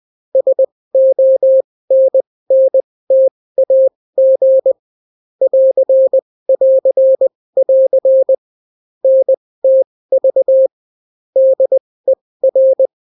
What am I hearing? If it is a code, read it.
Morse code, heard as a keyed tone, about 20 wpm, spaced as for 16 wpm, 530 Hz. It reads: SONNTAG <AR><AR><AR> NTV DER